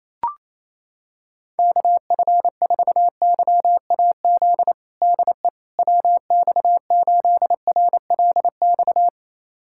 Morse code, heard as a keyed tone, about 28 wpm, 700 Hz. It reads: KF4YAZ DE WX8RLX
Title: USA Callsign DE Callsign 28wpm